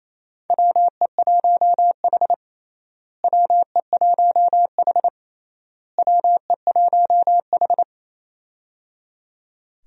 Morse code, heard as a keyed tone, about 27 wpm, 700 Hz. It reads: WE1H WE1H WE1H
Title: USA Call Signs - 4 Characters - Repeated 3x - 28wpm